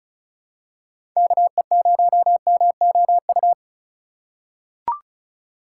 Morse code, heard as a keyed tone, about 36 wpm, 700 Hz. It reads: KE0MOU
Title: USA Call Signs - 2x Character Spacing - 35wpm